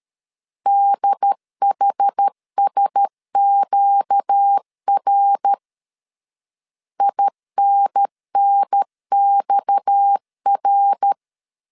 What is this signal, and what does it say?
Morse code, heard as a keyed tone, about 13 wpm, 800 Hz. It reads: DHSQR INNXR
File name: CW / DET / 1 6 1500C S B 12wpm 3431